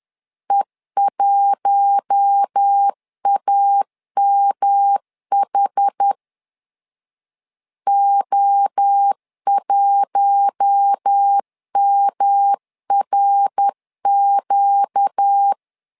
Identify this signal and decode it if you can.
Morse code, heard as a keyed tone, about 10 wpm, 800 Hz. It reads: E1AMH O1MRQ
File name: LEZIONI / 4 Q1 10wpm 2448